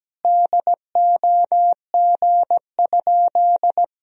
Morse code, heard as a keyed tone, about 17 wpm, 700 Hz. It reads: DOG?
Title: Sentences from Top 1000 Words - Review Part 2 - 17wpm